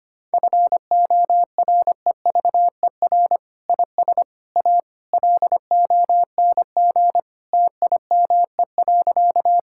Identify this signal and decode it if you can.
Morse code, heard as a keyed tone, about 25 wpm, 700 Hz. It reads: FOREVER IS A LONG TIME.
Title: Sentences from Top 1600 Words 25wpm